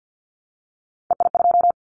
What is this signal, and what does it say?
Morse code, heard as a keyed tone, about 50 wpm, 700 Hz. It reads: EI2
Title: Callsign Prefixes for Top 50 DXCC - 50wpm